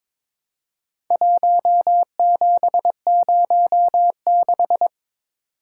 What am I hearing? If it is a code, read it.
Morse code, heard as a keyed tone, about 22 wpm, 700 Hz. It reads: 1706
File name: practice / 22wpm / Four Letter-Number Combinations - 22wpm